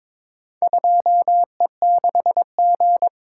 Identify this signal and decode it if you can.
Morse code, heard as a keyed tone, about 22 wpm, 700 Hz. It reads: 2E6G
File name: practice / 22wpm / Four Letter-Number Combinations - 22wpm